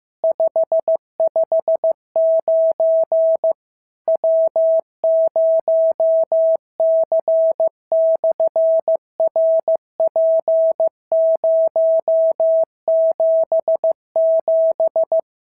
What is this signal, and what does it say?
Morse code, heard as a keyed tone, about 15 wpm, 650 Hz. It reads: 559 W0C/RP077
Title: SOTA QSO Practice 15wpm